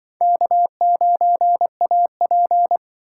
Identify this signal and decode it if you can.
Morse code, heard as a keyed tone, about 25 wpm, 700 Hz. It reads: K9AP